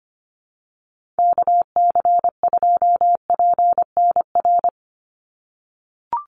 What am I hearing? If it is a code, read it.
Morse code, heard as a keyed tone, about 25 wpm, 700 Hz. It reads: KC2PNR
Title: USA Call Signs 25wpm